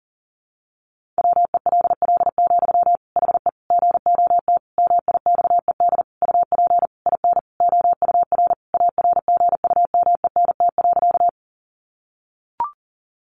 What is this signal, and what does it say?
Morse code, heard as a keyed tone, about 40 wpm, 700 Hz. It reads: WELL, HE GOT MIXED UP IN OUR ARGUMENT.
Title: Sentences from Top 1500 Words 40wpm